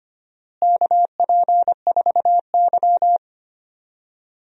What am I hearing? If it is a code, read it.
Morse code, heard as a keyed tone, about 25 wpm, 700 Hz. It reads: KP4Y
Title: DXCC-100 Call Signs - 4 Characters - Repeated 3x - 25wpm